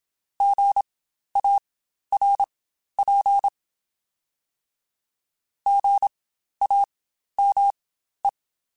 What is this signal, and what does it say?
Morse code, heard as a keyed tone, about 27 wpm, spaced as for 12 wpm, 800 Hz. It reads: GARP GAME
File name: Spaced Alphas / Spaced Alphas mp3s / Gs sp @25